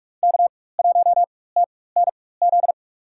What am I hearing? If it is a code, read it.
Morse code, heard as a keyed tone, about 46 wpm, spaced as for 21 wpm, 700 Hz. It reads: K1TNZ
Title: USA Call Signs - 5 Characters - 4x Character Spacing - 45wpm